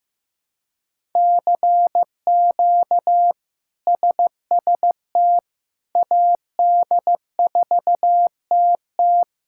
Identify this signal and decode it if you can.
Morse code, heard as a keyed tone, about 15 wpm, 700 Hz. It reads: CQ SST AD4TT
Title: K1USN SST QSO Practice - 15wpm